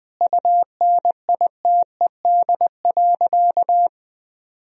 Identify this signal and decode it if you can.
Morse code, heard as a keyed tone, about 21 wpm, 700 Hz. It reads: UNITED.
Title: Sentences from Top 1000 Words - Review Part 4 - 20wpm